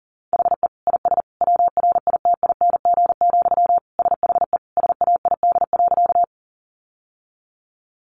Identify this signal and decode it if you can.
Morse code, heard as a keyed tone, about 40 wpm, 700 Hz. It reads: HE IS WRITING, SHE SAID.
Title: Sentences from Top 800 Words 40wpm